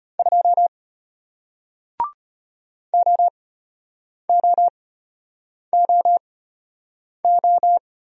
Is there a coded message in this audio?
Yes: Morse code from 0.2 to 7.8 s, heard as a keyed tone, about 33 wpm, spaced as for 8 wpm, 700 Hz. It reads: 2 OOOO